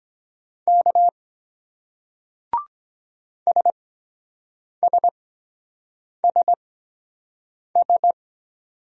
Morse code, heard as a keyed tone, about 23 wpm, spaced as for 7 wpm, 700 Hz. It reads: K SSSS